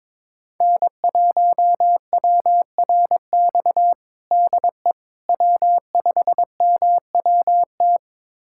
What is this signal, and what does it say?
Morse code, heard as a keyed tone, about 22 wpm, 700 Hz. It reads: N1WRX DE W5MWT
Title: USA Callsign DE Callsign - 5 Characters - 22wpm